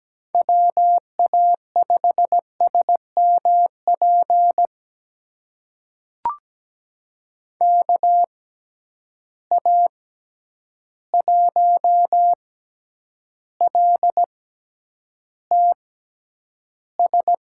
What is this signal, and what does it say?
Morse code, heard as a keyed tone, about 17 wpm, spaced as for 9 wpm, 700 Hz. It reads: WA5SMP KA1LTS
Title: USA Call Signs - 6 Characters - 6x Character Spacing - 17wpm